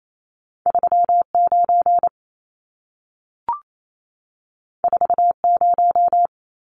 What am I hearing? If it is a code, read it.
Morse code, heard as a keyed tone, about 28 wpm, 700 Hz. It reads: 39 40
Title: Numbers - 2-Digit 28wpm